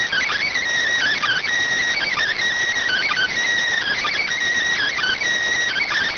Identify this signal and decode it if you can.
Emergency Alert System (SAME) tones, heard as harsh buzzing bursts. It is heard from 1.1 to 5.2 s.